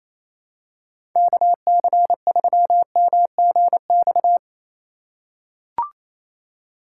Morse code, heard as a keyed tone, about 27 wpm, 700 Hz. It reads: KC3MGX
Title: USA Call Signs - 6 Characters - 3x Character Spacing - 28wpm